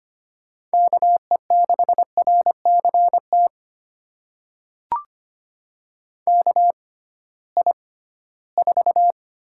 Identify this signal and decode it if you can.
Morse code, heard as a keyed tone, about 25 wpm, spaced as for 16 wpm, 700 Hz. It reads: KE6RCT KI4
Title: USA Call Signs - 6 Characters - 6x Character Spacing - 25wpm